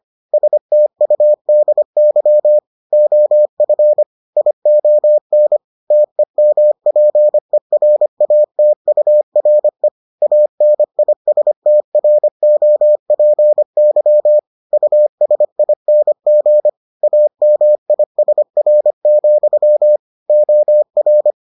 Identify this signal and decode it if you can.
Morse code, heard as a keyed tone, about 25 wpm, 600 Hz. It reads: STUDY OF ION TEMPERATURE ANISTROPY USING AMISR, OR